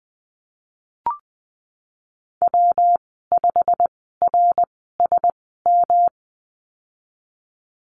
Morse code, heard as a keyed tone, about 19 wpm, spaced as for 14 wpm, 700 Hz. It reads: W5RSM
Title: 179 Lesson 29g - Slash - Call Signs 20wpm